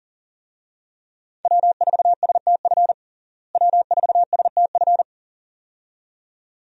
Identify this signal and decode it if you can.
Morse code, heard as a keyed tone, about 37 wpm, 700 Hz. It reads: W4STF W4STF
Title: USA Call Signs - 5 Characters - Repeated 2x - 40wpm